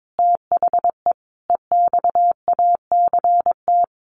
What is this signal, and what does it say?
Morse code, heard as a keyed tone, about 22 wpm, 700 Hz. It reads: THE EXACT
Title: Sentences from Top 1300 Words 22wpm